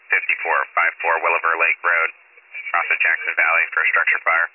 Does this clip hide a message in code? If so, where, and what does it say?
Yes: Emergency Alert System (SAME) tones from 0.6 to 3.4 s, heard as harsh buzzing bursts, with speech.